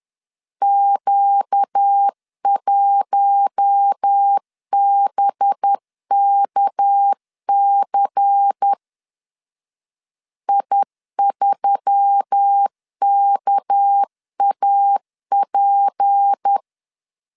Morse code, heard as a keyed tone, about 10 wpm, 800 Hz. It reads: Q1BKC I3KAP
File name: LEZIONI / 6 PKX0 / 6 PK 10wpm 2436